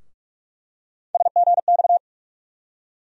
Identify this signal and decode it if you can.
Morse code, heard as a keyed tone, about 46 wpm, 700 Hz. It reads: SGX
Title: USA Call Sign Postfixes - 3 Letters 45wpm